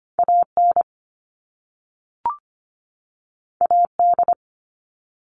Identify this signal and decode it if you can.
Morse code, heard as a keyed tone, about 25 wpm, 700 Hz. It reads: AN AD
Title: All Cut Numbers - 2-Digit 25wpm